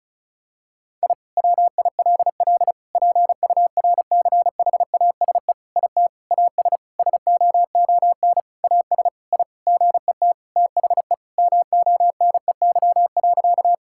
Morse code, heard as a keyed tone, about 34 wpm, 700 Hz. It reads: I WILL PURCHASE IT AS SOON AS I GET THE MONEY.
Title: Sentences from Top 1000 Words - Review Part 5 - 35wpm